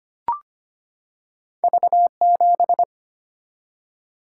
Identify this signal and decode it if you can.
Morse code, heard as a keyed tone, about 25 wpm, 700 Hz. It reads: V7
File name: Two Letter-Number Combinations - 25wpm